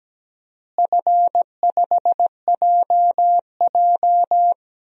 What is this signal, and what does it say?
Morse code, heard as a keyed tone, about 17 wpm, 700 Hz. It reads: F5JJ
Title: DXCC-100 Call Signs - 4 Characters - Repeated 3x - 17wpm